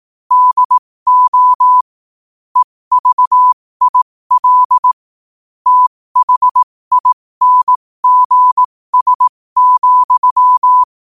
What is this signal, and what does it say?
Morse code, heard as a keyed tone, about 18 wpm, spaced as for 15 wpm, 1000 Hz. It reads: DO EVIL THINGS,
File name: qotdfiles / 15wpm